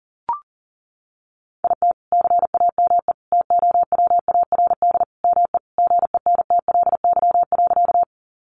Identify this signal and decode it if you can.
Morse code, heard as a keyed tone, about 40 wpm, 700 Hz. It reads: IT CAME TOWARD ME GENTLY.
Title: Sentences from Top 1500 Words 40wpm